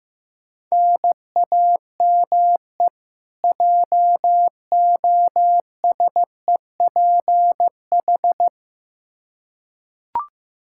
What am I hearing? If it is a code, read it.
Morse code, heard as a keyed tone, about 15 wpm, 700 Hz. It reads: NAME JOSEPH
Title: K1USN SST QSO Practice 15wpm